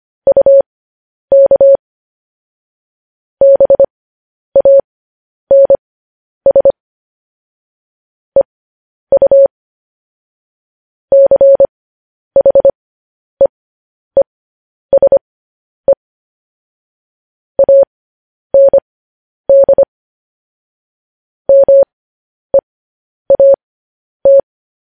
Morse code, heard as a keyed tone, about 25 wpm, spaced as for 10 wpm, 550 Hz. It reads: UK BANS EU CHEESE AND MEAT
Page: News Headlines in Morse Code at 10 WPM